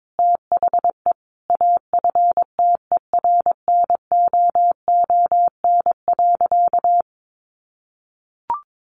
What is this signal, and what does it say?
Morse code, heard as a keyed tone, about 22 wpm, 700 Hz. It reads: THE AFTERNOON.
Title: Sentences from Top 1500 Words 22wpm